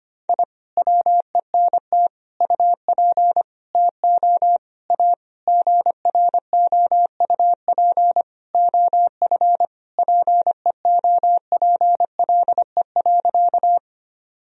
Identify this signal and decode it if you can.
Morse code, heard as a keyed tone, about 25 wpm, 700 Hz. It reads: I WENT UP TO A GROUP OF PEOPLE.
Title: Sentences from Top 1000 Words - Review Part 1 - 25wpm